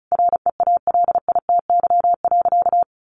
Morse code, heard as a keyed tone, about 34 wpm, 700 Hz. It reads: REALITY.
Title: Sentences from Top 1200 Words 35wpm